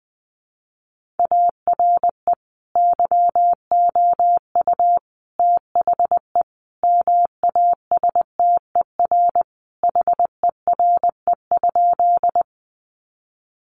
Morse code, heard as a keyed tone, about 20 wpm, 700 Hz. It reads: ARE YOU THE MASTER HERE?
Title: Sentences from Top 1300 Words 20wpm